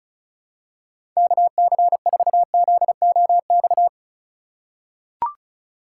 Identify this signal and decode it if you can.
Morse code, heard as a keyed tone, about 34 wpm, 700 Hz. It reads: KC4ZOX
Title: USA Call Signs - 6x Character Spacing - 35wpm